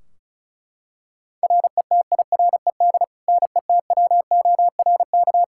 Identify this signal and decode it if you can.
Morse code, heard as a keyed tone, about 36 wpm, 700 Hz. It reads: RETIRED NETWORK
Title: K1USN SST QSO Practice 35wpm